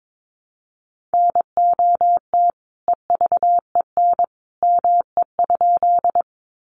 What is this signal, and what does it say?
Morse code, heard as a keyed tone, about 22 wpm, 700 Hz. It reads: NOT EVEN ME?
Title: Sentences from Top 600 Words 22wpm